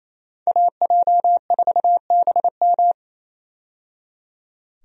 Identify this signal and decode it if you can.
Morse code, heard as a keyed tone, about 28 wpm, 700 Hz. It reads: AJ4BM
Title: USA Call Signs - 5 Characters - 28wpm